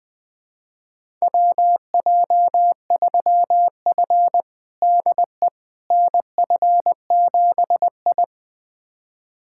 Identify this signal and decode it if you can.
Morse code, heard as a keyed tone, about 20 wpm, 700 Hz. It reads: WJ3F DE NF7I
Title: USA Callsign DE Callsign - 4 Characters - 20wpm